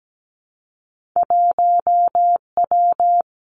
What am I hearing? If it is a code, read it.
Morse code, heard as a keyed tone, about 17 wpm, 700 Hz. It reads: 1 W
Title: Two Challenging Letter-Number Combinations 17wpm